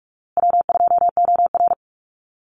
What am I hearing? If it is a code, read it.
Morse code, heard as a keyed tone, about 44 wpm, 700 Hz. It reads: W2OR
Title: Easy Call Signs Repeated 3x - 45wpm